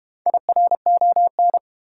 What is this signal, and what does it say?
Morse code, heard as a keyed tone, about 31 wpm, 700 Hz. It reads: IRON
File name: Easy Words - Speed-Racing with Hints - 25wpm